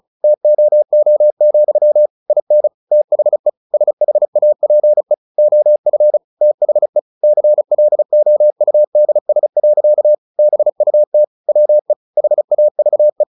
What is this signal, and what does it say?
Morse code, heard as a keyed tone, about 34 wpm, 600 Hz. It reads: TOO, IN THE SHAPE OF THE CLOUDS. BUT WE HAVE
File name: the old man and the sea / the old man and the sea 1 35wpm